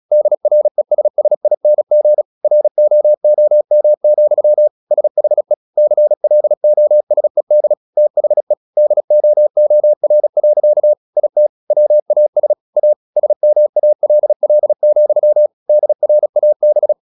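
Morse code, heard as a keyed tone, about 37 wpm, 600 Hz. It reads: DRESSING ROOM, SHE CLOSED THE DOOR. IT WAS A SMALL, DRAB
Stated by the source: Morse practice at 36 wpm, text of Ken Follett's novel Never, from CW med Gnister.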